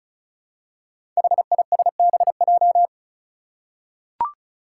Morse code, heard as a keyed tone, about 36 wpm, 700 Hz. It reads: HISBJ
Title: Five Challenging Letter-Number Combinations 35wpm